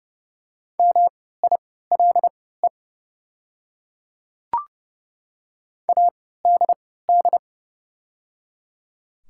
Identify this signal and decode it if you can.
Morse code, heard as a keyed tone, about 30 wpm, spaced as for 17 wpm, 700 Hz. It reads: MILE ADD
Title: 3x Character Spacing - Top 300 Words - 30wpm